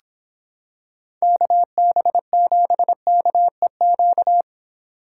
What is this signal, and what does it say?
Morse code, heard as a keyed tone, about 26 wpm, 700 Hz. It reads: KB7KEQ